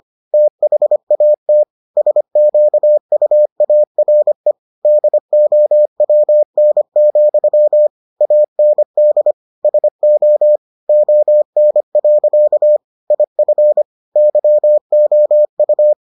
Morse code, heard as a keyed tone, about 25 wpm, 600 Hz. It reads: THAT SQUARE DOWN, AND SO ON. IF YOU